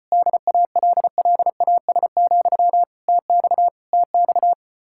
Morse code, heard as a keyed tone, about 33 wpm, 700 Hz. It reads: DALLAS, TX TX